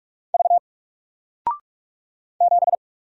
Morse code, heard as a keyed tone, about 44 wpm, 700 Hz. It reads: V 7
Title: Single Letter-Number Alt 45wpm